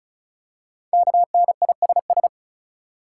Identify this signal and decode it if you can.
Morse code, heard as a keyed tone, about 36 wpm, 700 Hz. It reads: KNISS
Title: Code Groups - 5 Letters - 35wpm